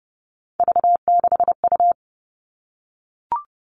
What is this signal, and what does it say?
Morse code, heard as a keyed tone, about 30 wpm, 700 Hz. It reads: V6U
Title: Three Challenging Letter-Number Combinations 30wpm